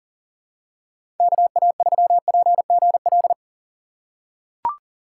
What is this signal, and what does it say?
Morse code, heard as a keyed tone, about 40 wpm, 700 Hz. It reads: KA3PGL
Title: USA Call Signs - 6 Characters - 6x Character Spacing - 40wpm